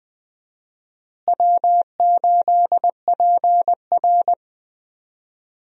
Morse code, heard as a keyed tone, about 20 wpm, 700 Hz. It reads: W8PR
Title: USA Call Signs - 4 Characters - Repeated 2x - 20wpm